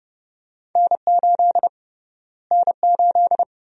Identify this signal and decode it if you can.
Morse code, heard as a keyed tone, about 30 wpm, 700 Hz. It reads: N8 N8
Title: USA Call Sign Prefixes - Repeated 2x - 30wpm